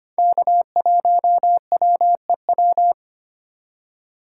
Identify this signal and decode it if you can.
Morse code, heard as a keyed tone, about 25 wpm, 700 Hz. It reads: K1WEW
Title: USA Call Signs - 5 Characters - 4x Character Spacing - 25wpm